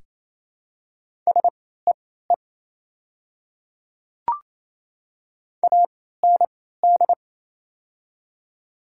Morse code, heard as a keyed tone, about 28 wpm, spaced as for 16 wpm, 700 Hz. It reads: SEE AND
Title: 3x Character Spacing - Top 200 Words - 28wpm